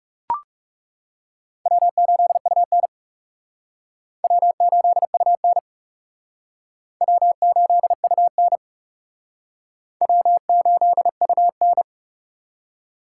Morse code, heard as a keyed tone, about 37 wpm, 700 Hz. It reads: W8UN W8UN W8UN W8UN
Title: Speed Racing - USA Call Signs 30wpm